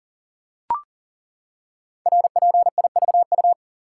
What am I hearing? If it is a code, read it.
Morse code, heard as a keyed tone, about 38 wpm, 700 Hz. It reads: RPIVU
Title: Code Groups - 5 Letters - 40wpm